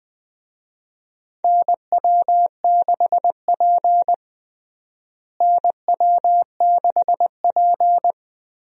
Morse code, heard as a keyed tone, about 20 wpm, 700 Hz. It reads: NW6P NW6P
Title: USA Call Signs - 4 Characters - Repeated 3x - 20wpm